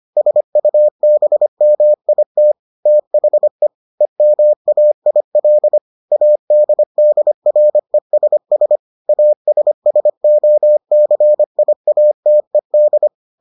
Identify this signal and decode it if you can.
Morse code, heard as a keyed tone, about 25 wpm, 600 Hz. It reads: SUBMIT THE EMAIL ADDRESS ASSOCIATED